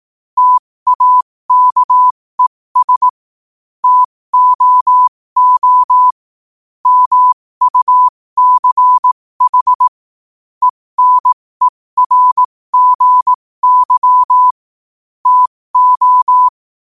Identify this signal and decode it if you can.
Morse code, heard as a keyed tone, about 18 wpm, spaced as for 15 wpm, 1000 Hz. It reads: TAKES TOO MUCH ENERGY TO
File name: qotdfiles / 15wpm